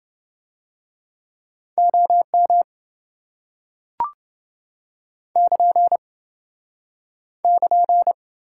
Morse code, heard as a keyed tone, about 28 wpm, 700 Hz. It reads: OM <KN> <KN>